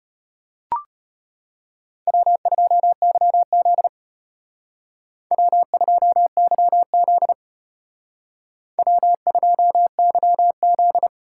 Morse code, heard as a keyed tone, about 30 wpm, 700 Hz. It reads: W2YZ W2YZ W2YZ
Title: Speed Racing - USA Call Signs 25wpm